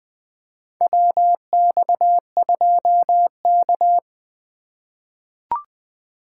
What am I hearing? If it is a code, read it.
Morse code, heard as a keyed tone, about 20 wpm, 700 Hz. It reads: WX2K
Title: USA Call Signs - 3x Character Spacing - 20wpm